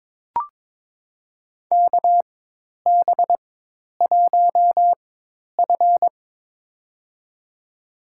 Morse code, heard as a keyed tone, about 22 wpm, spaced as for 10 wpm, 700 Hz. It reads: KB1F
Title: USA Call Signs - 4 Characters - 4x Character Spacing - 22wpm